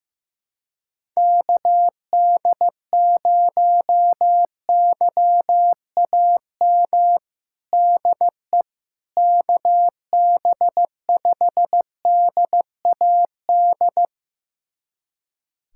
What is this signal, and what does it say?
Morse code, heard as a keyed tone, about 15 wpm, 700 Hz. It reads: KD0YAM DE KB5DAD
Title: USA Callsign DE Callsign - 6 Characters - 15wpm